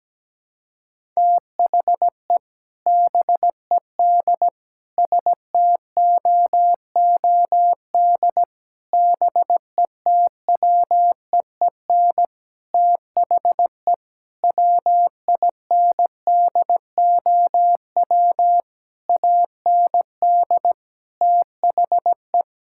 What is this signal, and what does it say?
Morse code, heard as a keyed tone, about 17 wpm, 700 Hz. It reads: THE BED STOOD BETWEEN THE WINDOW AND THE
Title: Sentences from Top 1600 Words 17wpm